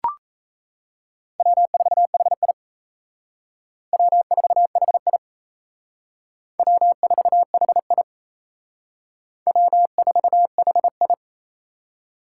Morse code, heard as a keyed tone, about 34 wpm, 700 Hz. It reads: W4HI W4HI W4HI W4HI
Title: Speed Racing - USA 1x2 Call Signs 28wpm